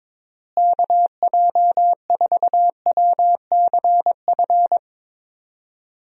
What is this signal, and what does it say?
Morse code, heard as a keyed tone, about 22 wpm, 700 Hz. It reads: KJ4WCF
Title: USA Call Signs - 6 Characters - Repeated 3x - 22wpm